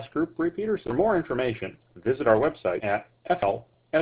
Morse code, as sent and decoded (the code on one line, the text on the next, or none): none